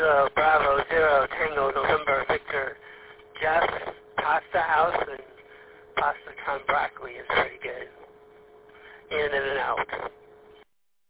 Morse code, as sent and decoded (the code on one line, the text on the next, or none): none